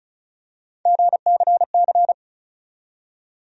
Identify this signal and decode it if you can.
Morse code, heard as a keyed tone, about 36 wpm, 700 Hz. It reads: GCC
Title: USA Call Sign Postfixes - 3 Letters 35wpm